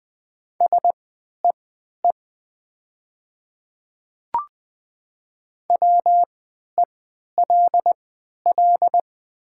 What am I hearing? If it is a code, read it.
Morse code, heard as a keyed tone, about 20 wpm, spaced as for 11 wpm, 700 Hz. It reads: SEE WELL
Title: 3x Character Spacing - Top 300 Words - 20wpm